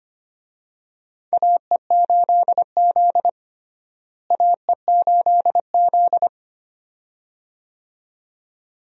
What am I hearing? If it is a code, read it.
Morse code, heard as a keyed tone, about 25 wpm, 700 Hz. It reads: AE8Z AE8Z
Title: USA Call Signs - 4 Characters - Repeated 2x - 25wpm